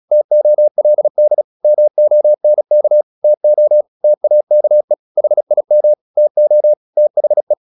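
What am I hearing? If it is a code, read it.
Morse code, heard as a keyed tone, about 36 wpm, 600 Hz. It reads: TOLD MONK TO TAKE HIM TO THE
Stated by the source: Morse practice at 36 wpm, text of Ken Follett's novel Never, from CW med Gnister.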